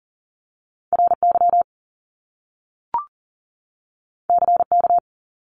Morse code, heard as a keyed tone, about 40 wpm, 700 Hz. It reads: RY CK
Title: Top Two Letter Combinations in Words 40wpm